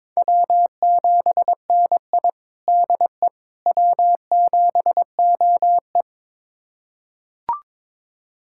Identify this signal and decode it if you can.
Morse code, heard as a keyed tone, about 21 wpm, 700 Hz. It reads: W7NI DE W7OE
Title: USA Callsign DE Callsign - 4 Characters - 22wpm